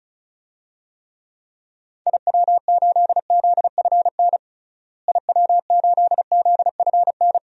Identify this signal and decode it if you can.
Morse code, heard as a keyed tone, about 36 wpm, 700 Hz. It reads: IW8ZFN IW8ZFN
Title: DXCC-100 Call Signs - 6 Characters - Repeated 3x - 35wpm